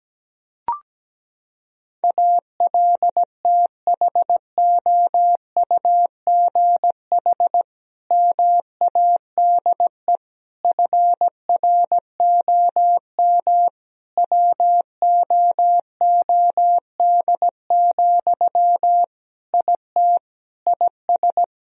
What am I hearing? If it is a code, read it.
Morse code, heard as a keyed tone, about 17 wpm, 700 Hz. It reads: ALTHOUGH MADE FROM WOOD, IT IS
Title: Sentences from Top 1000 Words - Review Part 3 - 17wpm